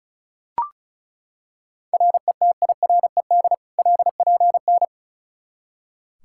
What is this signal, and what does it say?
Morse code, heard as a keyed tone, about 34 wpm, 700 Hz. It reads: RETIRED LPN
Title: Practice QSO - Exchange 3 - More Info - 35wpm